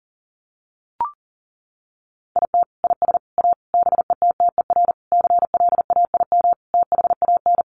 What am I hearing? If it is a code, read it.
Morse code, heard as a keyed tone, about 40 wpm, 700 Hz. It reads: IT IS A BETTER CLAIM THAN
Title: Sentences from Top 1500 Words 40wpm